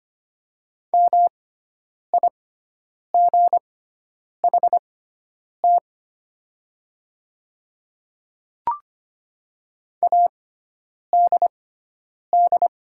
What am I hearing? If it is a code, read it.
Morse code, heard as a keyed tone, about 25 wpm, spaced as for 9 wpm, 700 Hz. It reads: MIGHT ADD
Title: 6x Character Spacing - Top 300 Words - 25wpm